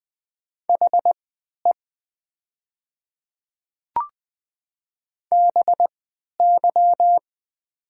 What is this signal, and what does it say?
Morse code, heard as a keyed tone, about 20 wpm, spaced as for 11 wpm, 700 Hz. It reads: HE BY
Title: 3x Character Spacing - Top 200 Words - 20wpm